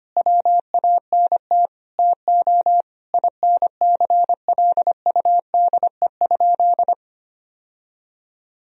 Morse code, heard as a keyed tone, about 25 wpm, 700 Hz. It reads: WANT TO INCLUDE?
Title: Sentences from Top 1000 Words - Review Part 3 - 25wpm